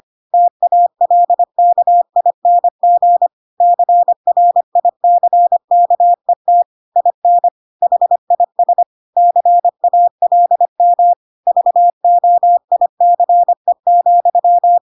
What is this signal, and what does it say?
Morse code, heard as a keyed tone, about 25 wpm, 700 Hz. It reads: TALKING CRICKET IN HIS CALM VOICE,
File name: Pinocchio / Pinocchio ch4 25 25 ik1hge